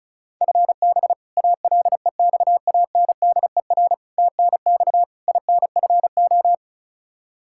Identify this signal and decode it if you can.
Morse code, heard as a keyed tone, about 36 wpm, 700 Hz. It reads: FB ALEXANDER TNX INFO
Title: K1USN SST QSO Practice 35wpm